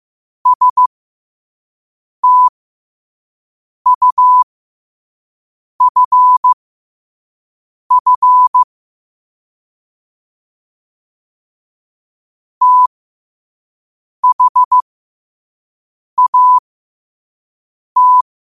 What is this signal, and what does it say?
Morse code, heard as a keyed tone, about 15 wpm, spaced as for 5 wpm, 1000 Hz. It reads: STUFF THAT
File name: qotdfiles / 5wpm